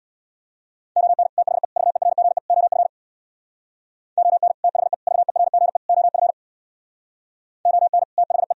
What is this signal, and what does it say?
Morse code, heard as a keyed tone, about 64 wpm, 700 Hz. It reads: ON THE VERGE OF ON THE VERGE OF ON THE